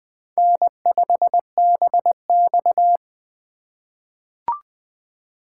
Morse code, heard as a keyed tone, about 20 wpm, 700 Hz. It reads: N5BX
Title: USA Call Signs - 4x Character Spacing - 20wpm